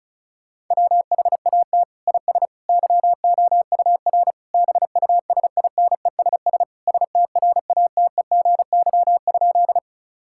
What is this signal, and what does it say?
Morse code, heard as a keyed tone, about 34 wpm, 700 Hz. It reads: WHAT IS YOUR BUSINESS STRATEGY?
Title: Sentences from Top 1000 Words - Review Part 5 - 35wpm